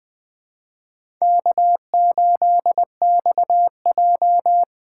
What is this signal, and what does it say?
Morse code, heard as a keyed tone, about 20 wpm, 700 Hz. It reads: K8XJ
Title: K1USN SST QSO Practice - 20wpm